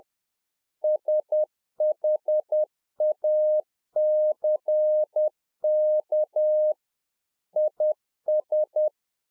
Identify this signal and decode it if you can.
Morse code, heard as a keyed tone, about 10 wpm, 610 Hz. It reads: SHACK IS